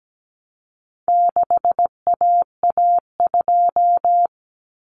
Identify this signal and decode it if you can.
Morse code, heard as a keyed tone, about 17 wpm, 700 Hz. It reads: 6AA2
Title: Common Cut Numbers - 4-Digit 17wpm